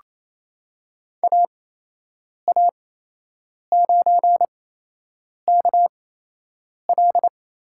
Morse code, heard as a keyed tone, about 27 wpm, spaced as for 8 wpm, 700 Hz. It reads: AA9KL
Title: USA Call Signs - 5 Characters - 8x Character Spacing - 28wpm